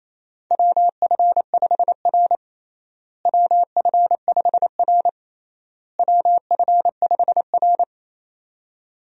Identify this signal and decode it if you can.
Morse code, heard as a keyed tone, about 27 wpm, 700 Hz. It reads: WF5R WF5R WF5R
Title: USA Call Signs - 4 Characters - Repeated 3x - 28wpm